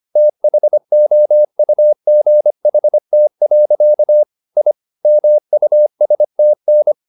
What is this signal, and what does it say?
Morse code, heard as a keyed tone, about 25 wpm, 600 Hz. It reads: THOUGHT. I MUSTN